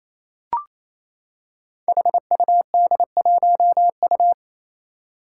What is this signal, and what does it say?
Morse code, heard as a keyed tone, about 28 wpm, 700 Hz. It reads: HUD1U
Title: Five Challenging Letter-Number Combinations 28wpm